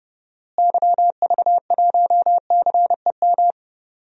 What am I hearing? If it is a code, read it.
Morse code, heard as a keyed tone, about 30 wpm, 700 Hz. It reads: YV1CEM
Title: DXCC-100 Call Signs - 6 Characters - Repeated 3x - 30wpm